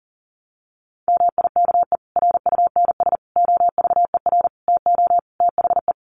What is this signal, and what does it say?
Morse code, heard as a keyed tone, about 40 wpm, 700 Hz. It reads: MIKE RUNS OVER TO THE
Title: Additional Easy Phrases 40wpm